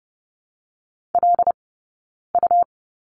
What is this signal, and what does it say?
Morse code, heard as a keyed tone, about 30 wpm, spaced as for 9 wpm, 690 Hz. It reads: LU
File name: Lesson 17d - Letter u - Two Characters 30wpm